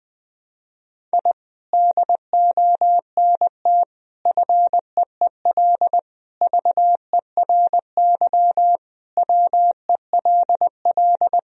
Morse code, heard as a keyed tone, about 20 wpm, 700 Hz. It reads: I DONT FEEL VERY WELL
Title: Sentences from Top 1000 Words - Review Part 3 - 20wpm